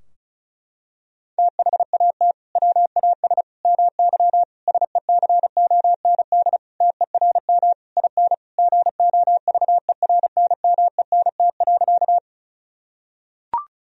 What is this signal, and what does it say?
Morse code, heard as a keyed tone, about 36 wpm, 700 Hz. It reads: THAT WAS MY SECOND TERM IN GOVERNMENT.
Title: Sentences from Top 1000 Words - Review Part 4 - 35wpm